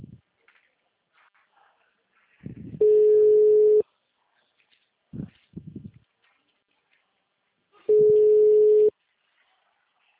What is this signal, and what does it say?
Ringback tone (Europe, 425 Hz).